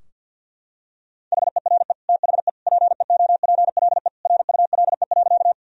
Morse code, heard as a keyed tone, about 49 wpm, 700 Hz. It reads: HERE THE PEOPLE RULE.
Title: Sentences from Top 1000 Words - Review Part 2 - 50wpm